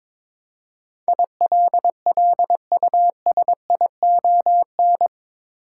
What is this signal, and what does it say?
Morse code, heard as a keyed tone, about 22 wpm, 700 Hz. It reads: ILLUSION
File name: practice / 22wpm / Eight Letter Words - 22wpm